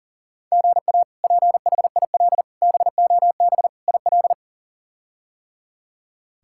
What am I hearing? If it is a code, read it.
Morse code, heard as a keyed tone, about 40 wpm, 700 Hz. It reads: GA PHIL BOB IL
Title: K1USN SST QSO Practice - 40wpm